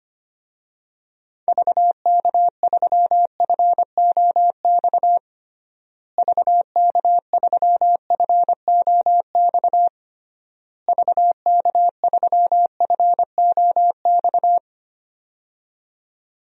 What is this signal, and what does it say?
Morse code, heard as a keyed tone, about 25 wpm, 700 Hz. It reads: VK3FOX VK3FOX VK3FOX
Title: DXCC-100 Call Signs - 6 Characters - Repeated 3x - 25wpm